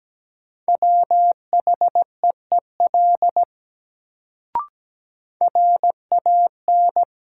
Morse code, heard as a keyed tone, about 17 wpm, 700 Hz. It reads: WHEEL RAN
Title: Sentences from Top 500 Words - New Word Drill - 17wpm